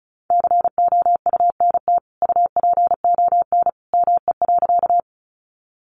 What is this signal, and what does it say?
Morse code, heard as a keyed tone, about 34 wpm, 700 Hz. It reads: COUNT UPON ME.
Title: Sentences from Top 1500 Words 35wpm